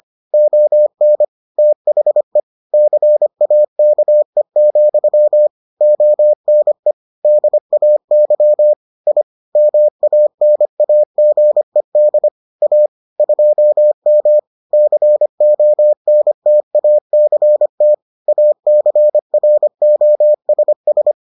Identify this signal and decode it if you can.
Morse code, heard as a keyed tone, about 25 wpm, 600 Hz. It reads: ON THE CAKE, ONE DAY I MANAGED A 2M CONTACT ACROSS